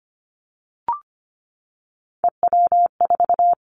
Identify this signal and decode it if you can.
Morse code, heard as a keyed tone, about 25 wpm, 700 Hz. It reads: EW4
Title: Callsign Prefixes for Top 50 DXCC - 25wpm